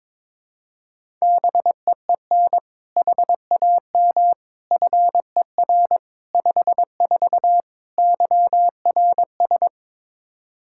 Morse code, heard as a keyed tone, about 21 wpm, 700 Hz. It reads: BEEN HAM FER 54 YRS
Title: Practice QSO - Exchange 3 - More Info - 22wpm